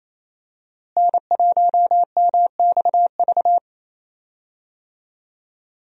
Morse code, heard as a keyed tone, about 27 wpm, 700 Hz. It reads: N1MXV
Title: USA Call Signs - 5 Characters - Repeated 3x - 28wpm